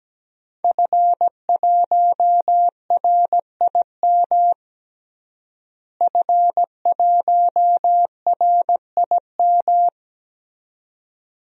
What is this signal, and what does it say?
Morse code, heard as a keyed tone, about 17 wpm, 700 Hz. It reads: F1RIM F1RIM
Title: DXCC-100 Call Signs - 5 Characters - Repeated 2x - 17wpm